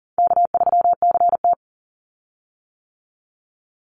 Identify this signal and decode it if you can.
Morse code, heard as a keyed tone, about 37 wpm, 700 Hz. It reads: K3CT
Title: Easy Call Signs - 40wpm